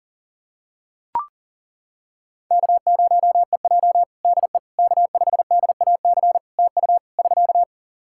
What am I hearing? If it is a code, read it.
Morse code, heard as a keyed tone, about 37 wpm, 700 Hz. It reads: K0EJ DE K5DAC TU <SK>
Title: Practice QSO - Exchange 4 - Ending - 40wpm